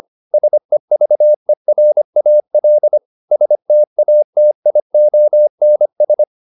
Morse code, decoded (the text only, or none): SEVERAL STATIONS